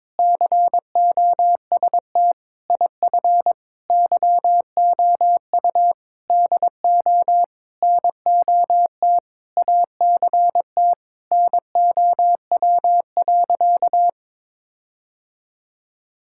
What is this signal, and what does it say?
Morse code, heard as a keyed tone, about 22 wpm, 700 Hz. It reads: COST IF YOU DO NOT ACT NOW.
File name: practice / 22wpm / Sentences with 8-Word Phrases - 22wpm